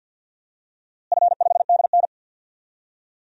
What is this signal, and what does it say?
Morse code, heard as a keyed tone, about 46 wpm, 700 Hz. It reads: F5DN
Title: DXCC-100 Call Signs - 4 Characters - Repeated 2x - 50wpm